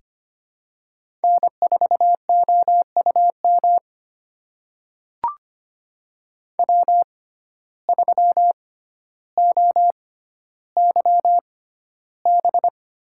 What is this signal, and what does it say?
Morse code, heard as a keyed tone, about 25 wpm, spaced as for 13 wpm, 700 Hz. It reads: N4OUM W3OYB
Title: USA Call Signs - 5 Characters - 6x Character Spacing - 25wpm